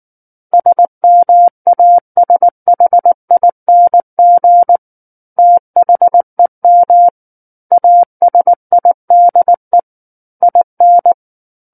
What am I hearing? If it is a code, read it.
Morse code, heard as a keyed tone, about 19 wpm, 700 Hz. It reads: SMASHING THEM ASIDE IN
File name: Book One / 10-Chapter 10 - 19 WPM